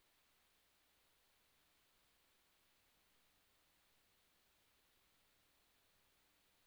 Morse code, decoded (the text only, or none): none